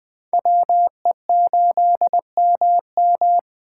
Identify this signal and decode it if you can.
Morse code, heard as a keyed tone, about 21 wpm, 700 Hz. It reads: WE8MM